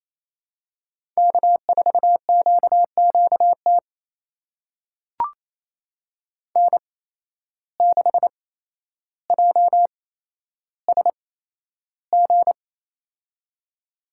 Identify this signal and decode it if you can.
Morse code, heard as a keyed tone, about 28 wpm, spaced as for 12 wpm, 700 Hz. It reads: K4QQT N6JSG
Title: USA Call Signs - 5 Characters - 8x Character Spacing - 28wpm